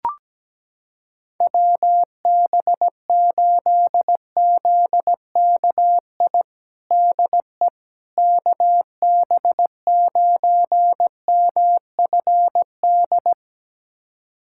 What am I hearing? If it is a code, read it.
Morse code, heard as a keyed tone, about 17 wpm, 700 Hz. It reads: WB8ZKI DE KB9MFD
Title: USA Callsign DE Callsign - 6 Characters - 17wpm